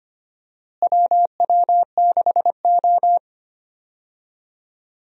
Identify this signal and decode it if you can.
Morse code, heard as a keyed tone, about 25 wpm, 700 Hz. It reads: WW6O
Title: USA Call Signs - 4 Characters - Repeated 3x - 25wpm